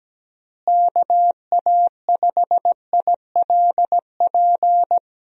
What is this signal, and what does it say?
Morse code, heard as a keyed tone, about 17 wpm, 700 Hz. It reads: KA5ILP
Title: USA Call Signs - 6 Characters - Repeated 2x - 17wpm